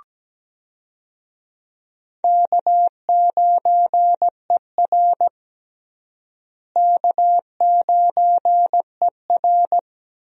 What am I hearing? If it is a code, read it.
Morse code, heard as a keyed tone, about 17 wpm, 700 Hz. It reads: K9ER K9ER
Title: USA Call Signs - 4 Characters - Repeated 3x - 17wpm